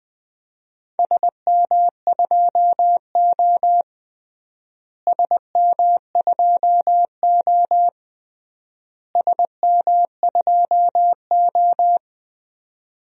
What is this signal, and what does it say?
Morse code, heard as a keyed tone, about 21 wpm, 700 Hz. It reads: SM2O SM2O SM2O
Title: DXCC-100 Call Signs - 4 Characters - Repeated 3x - 20wpm